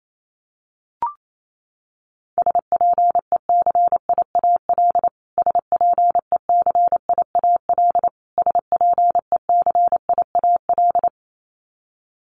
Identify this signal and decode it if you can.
Morse code, heard as a keyed tone, about 28 wpm, 700 Hz. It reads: SPECIAL SPECIAL SPECIAL
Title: Top 500 Words Repeated 28wpm